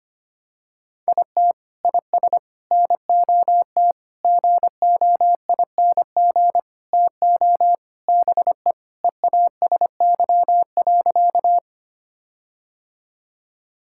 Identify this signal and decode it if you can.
Morse code, heard as a keyed tone, about 25 wpm, 700 Hz. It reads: IT IS NOT GOING TO BE EASY.
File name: practice / 25wpm / Sentences with 4-Word Phrases - 25wpm